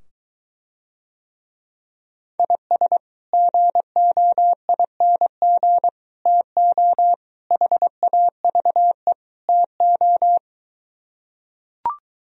Morse code, heard as a keyed tone, about 23 wpm, 700 Hz. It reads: IS GOING TO HAVE TO